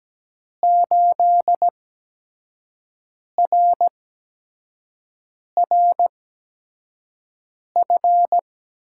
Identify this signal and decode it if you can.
Morse code, heard as a keyed tone, about 17 wpm, spaced as for 5 wpm, 700 Hz. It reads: 8RRF
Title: USA Call Signs - 5 Characters - 8x Character Spacing - 17wpm